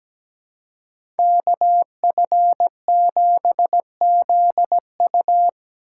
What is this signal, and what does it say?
Morse code, heard as a keyed tone, about 17 wpm, 700 Hz. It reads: KF7ZU
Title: USA Call Signs - 5 Characters - 4x Character Spacing - 17wpm